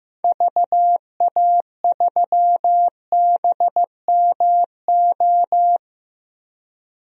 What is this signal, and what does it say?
Morse code, heard as a keyed tone, about 15 wpm, 700 Hz. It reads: VA3BMO
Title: DXCC-100 Call Signs - 6 Characters - 15wpm